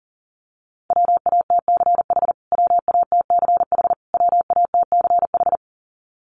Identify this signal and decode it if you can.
Morse code, heard as a keyed tone, about 40 wpm, 700 Hz. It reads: WATCH WATCH WATCH
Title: Top 500 Words Repeated 40wpm